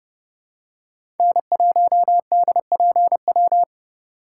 Morse code, heard as a keyed tone, about 32 wpm, 700 Hz. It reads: N1DPW
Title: USA Call Signs - 5 Characters - 2x Character Spacing - 30wpm